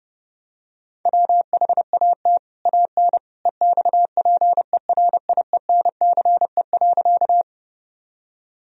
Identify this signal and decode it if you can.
Morse code, heard as a keyed tone, about 32 wpm, 700 Hz. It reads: WHAT AN EXPERIENCE.
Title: Sentences from Top 1000 Words - Review Part 3 - 30wpm